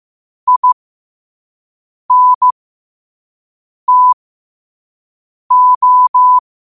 Morse code, heard as a keyed tone, about 15 wpm, spaced as for 5 wpm, 1000 Hz. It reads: INTO
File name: qotdfiles / 5wpm